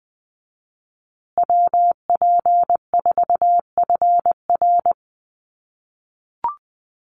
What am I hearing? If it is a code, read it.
Morse code, heard as a keyed tone, about 20 wpm, 700 Hz. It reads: WP4FR
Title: USA Call Signs Repeated 2x 20wpm